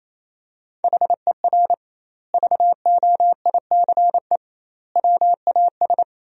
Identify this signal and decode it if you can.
Morse code, heard as a keyed tone, about 28 wpm, 700 Hz. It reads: HER VOICE WAS
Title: 2x Word-Spacing - Sentences from Top 500 Words - 28wpm